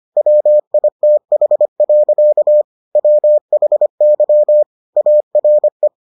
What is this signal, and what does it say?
Morse code, heard as a keyed tone, about 25 wpm, 600 Hz. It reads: WITH. WHY ARE